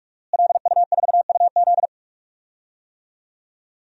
Morse code, heard as a keyed tone, about 46 wpm, 700 Hz. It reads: LU4UZ
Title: DXCC-100 Call Signs - 5 Characters - Repeated 2x - 45wpm